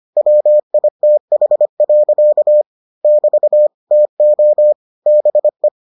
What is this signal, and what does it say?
Morse code, heard as a keyed tone, about 25 wpm, 600 Hz. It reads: WITH. <BT> TO BE